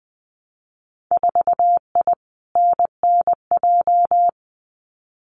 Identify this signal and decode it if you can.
Morse code, heard as a keyed tone, about 20 wpm, 700 Hz. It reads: 4I NNJ
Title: Winter Field Day Exchanges 20wpm